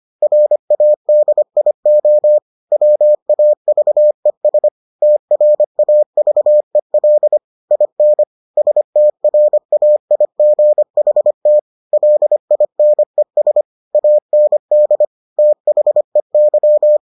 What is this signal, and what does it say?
Morse code, heard as a keyed tone, about 25 wpm, 600 Hz. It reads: RADIO WAVES TRAVEL IN STRAIGHT LINES AND THEY